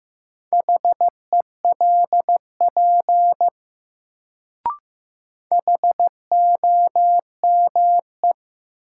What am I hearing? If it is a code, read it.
Morse code, heard as a keyed tone, about 15 wpm, 700 Hz. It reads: HELP HOME
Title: Sentences from Top 200 Words - New Word Drill - 15wpm